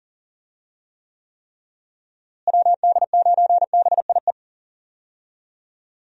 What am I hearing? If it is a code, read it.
Morse code, heard as a keyed tone, about 40 wpm, 700 Hz. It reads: WD9BIE
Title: USA Call Signs - 6 Characters - Repeated 3x - 40wpm